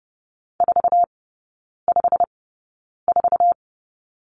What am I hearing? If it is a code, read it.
Morse code, heard as a keyed tone, about 30 wpm, spaced as for 9 wpm, 700 Hz. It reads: 454
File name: course / 30:10wpm / 087 Lesson 15e - Number 5 - Three Characters 30wpm